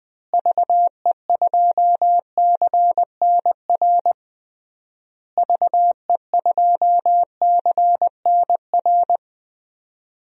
Morse code, heard as a keyed tone, about 21 wpm, 700 Hz. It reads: VE2CNR VE2CNR
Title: DXCC-100 Call Signs - 6 Characters - Repeated 3x - 20wpm